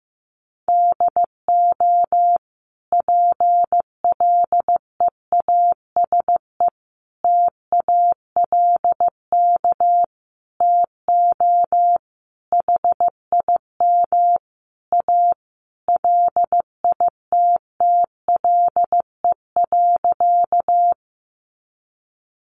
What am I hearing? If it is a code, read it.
Morse code, heard as a keyed tone, about 15 wpm, 700 Hz. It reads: DO PLEASE TALK TO HIM A LITTLE.
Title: Sentences from Top 1500 Words 15wpm